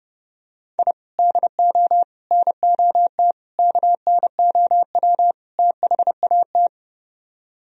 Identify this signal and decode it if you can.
Morse code, heard as a keyed tone, about 30 wpm, 700 Hz. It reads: I DO NOT KNOW THAT